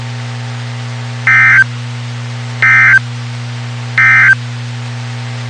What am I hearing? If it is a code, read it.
Emergency Alert System (SAME) tones, heard as harsh buzzing bursts. It is heard from 1.2 to 4.4 s.
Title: WXK91 NOAA Weather Radio